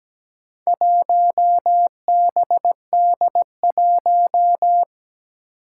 Morse code, heard as a keyed tone, about 17 wpm, 700 Hz. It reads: 1BD1
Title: Four Challenging Letter-Number Combinations 17wpm